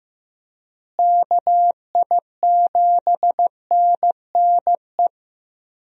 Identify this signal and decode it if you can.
Morse code, heard as a keyed tone, about 15 wpm, 700 Hz. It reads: KI7NNE
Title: USA Call Signs - 6 Characters - Repeated 2x - 15wpm